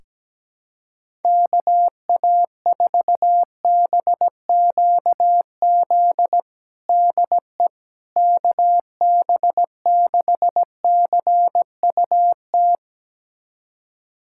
Morse code, heard as a keyed tone, about 17 wpm, 700 Hz. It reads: KA4BQZ DE KB6CUT
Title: USA Callsign DE Callsign - 6 Characters - 17wpm